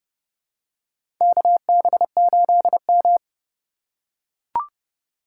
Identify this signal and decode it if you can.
Morse code, heard as a keyed tone, about 31 wpm, 700 Hz. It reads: KB8M
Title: K1USN SST QSO Practice - 30wpm